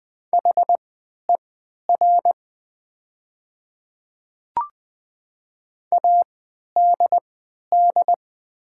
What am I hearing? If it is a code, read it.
Morse code, heard as a keyed tone, about 20 wpm, spaced as for 11 wpm, 700 Hz. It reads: HER ADD
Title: 3x Character Spacing - Top 200 Words - 20wpm